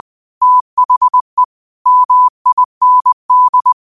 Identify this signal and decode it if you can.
Morse code, heard as a keyed tone, about 20 wpm, 1000 Hz. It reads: THE MIND
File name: qotdfiles / 20wpm